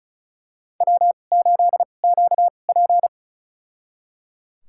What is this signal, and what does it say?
Morse code, heard as a keyed tone, about 36 wpm, spaced as for 25 wpm, 700 Hz. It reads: W8QP
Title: USA Call Signs - 4 Characters - 2x Character Spacing - 35wpm